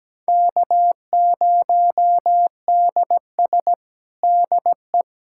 Morse code, heard as a keyed tone, about 17 wpm, 700 Hz. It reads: K0DS DE
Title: USA Callsign DE Callsign - 4 Characters - 17wpm